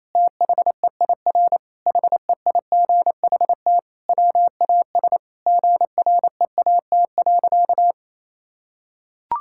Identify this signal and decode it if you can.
Morse code, heard as a keyed tone, about 27 wpm, 700 Hz. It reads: THEIR HEIGHT WAS GREAT.
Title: Sentences from Top 1600 Words 28wpm